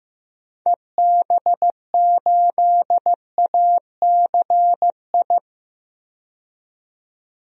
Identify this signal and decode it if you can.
Morse code, heard as a keyed tone, about 15 wpm, 700 Hz. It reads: EB8ACI
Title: DXCC-100 Call Signs - 6 Characters - 15wpm